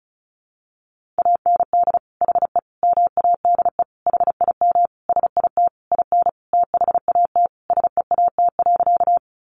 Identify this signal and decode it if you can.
Morse code, heard as a keyed tone, about 34 wpm, 700 Hz. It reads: AND HE MADE HIM SIT IN THAT SEAT.
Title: Sentences from Top 1500 Words 35wpm